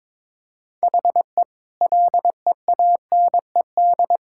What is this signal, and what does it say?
Morse code, heard as a keyed tone, about 21 wpm, 700 Hz. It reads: HE LEANED
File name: practice / 22wpm / Sentences with 2-Word Phrases - 22wpm